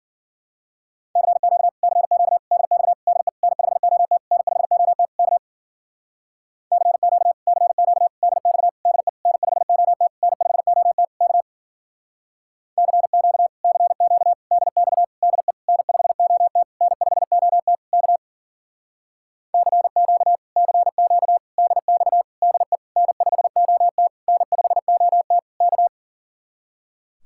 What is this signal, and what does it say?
Morse code, heard as a keyed tone, about 51 wpm, 700 Hz. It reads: CQ CQ DX DE N5OT N5OT K CQ CQ DX DE N5OT N5OT K CQ CQ DX DE N5OT N5OT K CQ CQ DX DE N5OT N5OT K